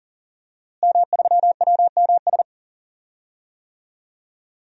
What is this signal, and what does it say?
Morse code, heard as a keyed tone, about 40 wpm, 700 Hz. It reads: M3WMS
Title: DXCC-100 Call Signs - 5 Characters - Repeated 3x - 40wpm